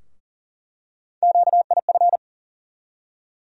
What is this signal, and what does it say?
Morse code, heard as a keyed tone, about 40 wpm, 700 Hz. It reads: QIF
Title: USA Call Sign Postfixes - 3 Letters 40wpm